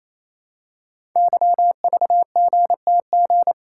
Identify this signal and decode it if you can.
Morse code, heard as a keyed tone, about 27 wpm, 700 Hz. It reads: YVGTG
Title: Code Groups - 5 Letters - 28wpm